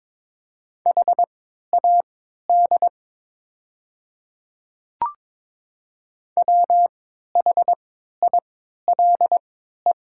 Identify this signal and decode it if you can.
Morse code, heard as a keyed tone, about 22 wpm, spaced as for 12 wpm, 700 Hz. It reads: HAD WHILE
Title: 3x Character Spacing - Top 300 Words - 22wpm